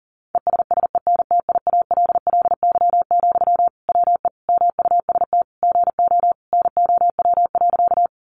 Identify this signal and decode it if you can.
Morse code, heard as a keyed tone, about 40 wpm, 700 Hz. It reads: ESSENTIALLY, WE MUST GO NOW.
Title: Sentences from Top 1300 Words 40wpm